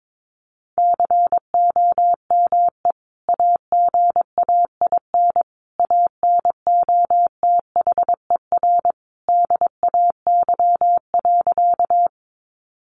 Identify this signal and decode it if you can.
Morse code, heard as a keyed tone, about 22 wpm, 700 Hz. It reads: COME AGAIN ANOTHER DAY.
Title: Sentences from Top 600 Words 22wpm